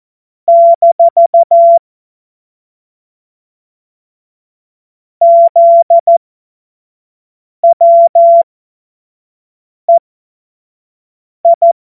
Morse code, heard as a keyed tone, about 14 wpm, spaced as for 5 wpm, 680 Hz. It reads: - ZWEI
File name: Witz-16-September-2025-25-F70